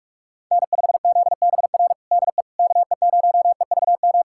Morse code, heard as a keyed tone, about 43 wpm, 700 Hz. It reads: N5ZBR DE KE0EVM
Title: USA Callsign DE Callsign 45wpm